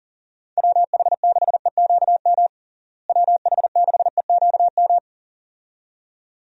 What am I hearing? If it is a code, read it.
Morse code, heard as a keyed tone, about 40 wpm, 700 Hz. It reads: WH6EQM WH6EQM
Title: USA Call Signs - 6 Characters - Repeated 2x - 40wpm